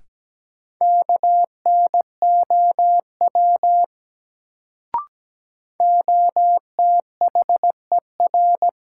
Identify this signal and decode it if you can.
Morse code, heard as a keyed tone, about 17 wpm, 700 Hz. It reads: KNOW OTHER
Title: Sets of 2 Words - New Word Drill - 17wpm